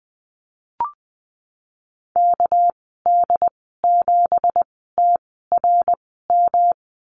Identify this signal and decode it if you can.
Morse code, heard as a keyed tone, about 20 wpm, spaced as for 14 wpm, 700 Hz. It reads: KD7TRM
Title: 200 Lesson 32g - Letter k - Call Signs 20wpm